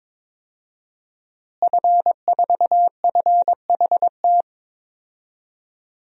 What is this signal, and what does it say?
Morse code, heard as a keyed tone, about 22 wpm, 700 Hz. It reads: F4FHT
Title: DXCC-100 Call Signs - 5 Characters - Repeated 2x - 22wpm